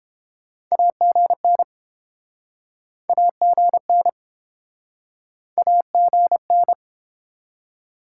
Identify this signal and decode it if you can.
Morse code, heard as a keyed tone, about 28 wpm, 700 Hz. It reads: AGN AGN AGN